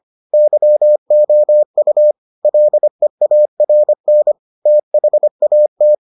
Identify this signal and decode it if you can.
Morse code, heard as a keyed tone, about 25 wpm, 600 Hz. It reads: YOU LEARN THAT